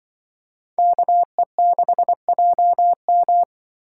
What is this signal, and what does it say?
Morse code, heard as a keyed tone, about 24 wpm, 700 Hz. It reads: KE6JM
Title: Speed Racing - USA Call Signs 20wpm